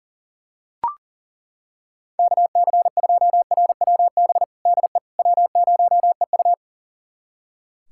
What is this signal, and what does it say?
Morse code, heard as a keyed tone, about 40 wpm, 700 Hz. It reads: KC2RWB DE W0EU
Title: USA Callsign DE Callsign 40wpm